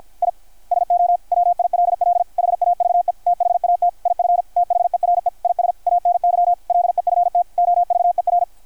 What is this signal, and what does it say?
Morse code, heard as a keyed tone, about 53 wpm, 700 Hz, with radio noise. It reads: I SO WILL HAVE THAT IF THERE IS ANY LEFT OVER
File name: archived / CRSnet